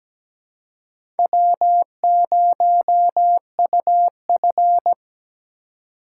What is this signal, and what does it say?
Morse code, heard as a keyed tone, about 17 wpm, 700 Hz. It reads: W0UF
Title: K1USN SST QSO Practice - 17wpm